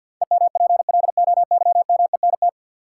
Morse code, heard as a keyed tone, about 53 wpm, 700 Hz. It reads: EMPLOYMENT
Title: Sentences from Top 1500 Words - New Word Drill Alt - 50wpm